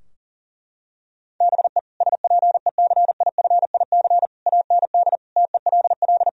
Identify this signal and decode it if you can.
Morse code, heard as a keyed tone, about 40 wpm, 700 Hz. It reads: BE SPECIFIC AND TELL
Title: Sentences from Top 1000 Words - Review Part 3 - 40wpm